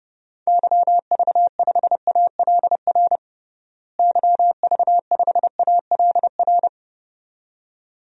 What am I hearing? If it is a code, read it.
Morse code, heard as a keyed tone, about 30 wpm, 700 Hz. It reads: YV5ALR YV5ALR
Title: DXCC-100 Call Signs - 6 Characters - Repeated 2x - 30wpm